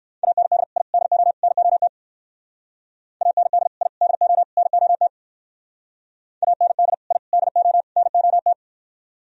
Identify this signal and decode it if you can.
Morse code, heard as a keyed tone, about 60 wpm, 700 Hz. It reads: AND I DO NOT AND I DO NOT AND I DO NOT